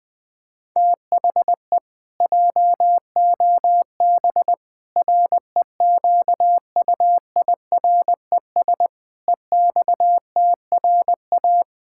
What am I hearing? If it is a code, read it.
Morse code, heard as a keyed tone, about 21 wpm, 700 Hz. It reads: THE JOB REQUIRES EXTRA
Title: Pangram Challenge v2 - 20wpm